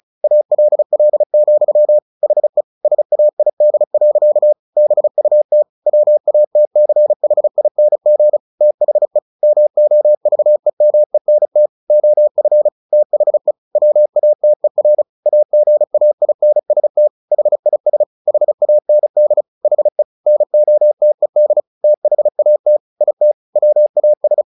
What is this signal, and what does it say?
Morse code, heard as a keyed tone, about 34 wpm, 600 Hz. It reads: ALL, HE SAID. BUT WATCHING THE MOVEMENT OF THE WATER AGAINST HIS HAND HE NOTED THAT IT WAS
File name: the old man and the sea 2 35wpm